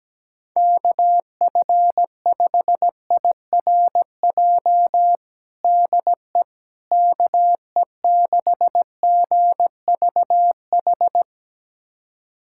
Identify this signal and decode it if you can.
Morse code, heard as a keyed tone, about 17 wpm, 700 Hz. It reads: KF5IRJ DE KE6GVH
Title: USA Callsign DE Callsign - 6 Characters - 17wpm